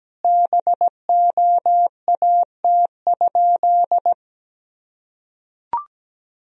Morse code, heard as a keyed tone, about 17 wpm, 700 Hz. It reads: BOAT?
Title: Sentences from Top 1000 Words - Review Part 3 - 17wpm